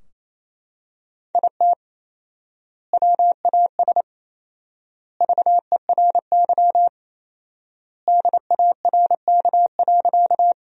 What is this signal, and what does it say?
Morse code, heard as a keyed tone, about 28 wpm, 700 Hz. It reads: IT WAS VERY DARK.
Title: 4x Word-Spacing - Sentences from Top 500 Words - 28wpm